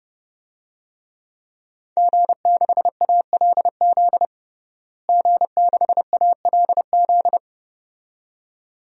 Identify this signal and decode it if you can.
Morse code, heard as a keyed tone, about 30 wpm, 700 Hz. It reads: G6ALZ G6ALZ
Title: DXCC-100 Call Signs - 5 Characters - Repeated 2x - 30wpm